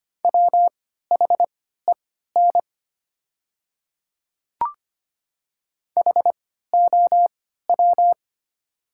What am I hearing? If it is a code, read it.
Morse code, heard as a keyed tone, about 25 wpm, spaced as for 14 wpm, 700 Hz. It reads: WHEN HOW
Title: 3x Character Spacing - Top 100 Words - 25wpm